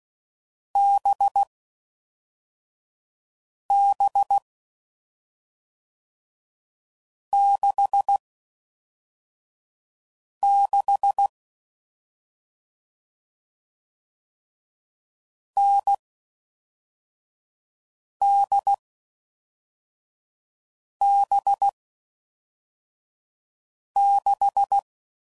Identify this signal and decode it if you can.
Morse code, heard as a keyed tone, about 16 wpm, 800 Hz. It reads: B B 6 6 N D B 6
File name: Mixed / Short Runs / NDB NDB6